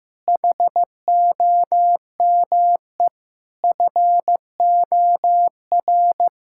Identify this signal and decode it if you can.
Morse code, heard as a keyed tone, about 15 wpm, 700 Hz. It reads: HOME FOR
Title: Sentences from Top 1000 Words - Review Part 5 - 15wpm